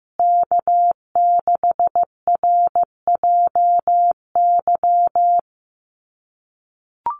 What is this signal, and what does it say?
Morse code, heard as a keyed tone, about 15 wpm, 700 Hz. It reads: K6RJY
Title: USA Call Signs Repeated 2x 15wpm